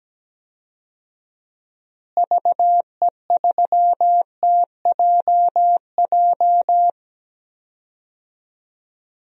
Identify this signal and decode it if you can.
Morse code, heard as a keyed tone, about 17 wpm, 700 Hz. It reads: VE3TJJ
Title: DXCC-100 Call Signs - 6 Characters - Repeated 3x - 17wpm